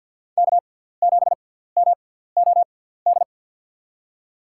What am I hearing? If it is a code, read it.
Morse code, heard as a keyed tone, about 52 wpm, spaced as for 17 wpm, 690 Hz. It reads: K7MOD
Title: USA Call Signs - 5 Characters - 6x Character Spacing - 50wpm